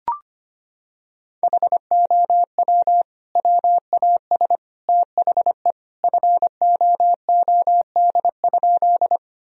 Morse code, heard as a keyed tone, about 25 wpm, 700 Hz. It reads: HOW WAS THE FOOD?
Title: Sentences from Top 1000 Words - Review Part 1 - 25wpm